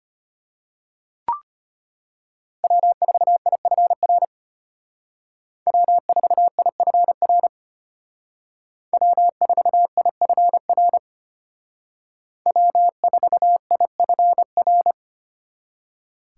Morse code, heard as a keyed tone, about 32 wpm, 700 Hz. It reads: W4IFR W4IFR W4IFR W4IFR